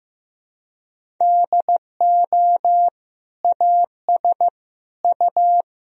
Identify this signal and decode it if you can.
Morse code, heard as a keyed tone, about 16 wpm, 700 Hz. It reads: DO AS U
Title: Sentences from Top 100 Words and Abbreviations 15wpm